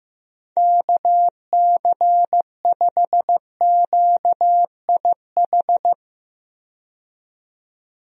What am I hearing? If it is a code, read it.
Morse code, heard as a keyed tone, about 16 wpm, 700 Hz. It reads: KC5QIH
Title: USA Call Signs - 6 Characters - 15wpm